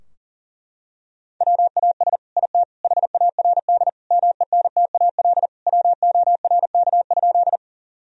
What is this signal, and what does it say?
Morse code, heard as a keyed tone, about 40 wpm, 700 Hz. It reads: WAS IT HARD MENTAL WORK?
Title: Sentences from Top 1000 Words - Review Part 4 - 40wpm